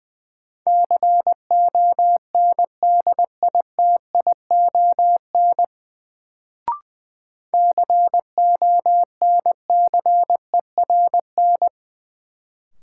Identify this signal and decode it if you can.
Morse code, heard as a keyed tone, about 20 wpm, 700 Hz. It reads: CONDITION CONCERN
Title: Sentences from Top 1000 Words - New Word Drill - 20wpm